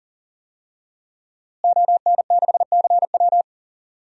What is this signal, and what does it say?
Morse code, heard as a keyed tone, about 40 wpm, 700 Hz. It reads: ON6CW
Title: DXCC-100 Call Signs - 5 Characters - Repeated 3x - 40wpm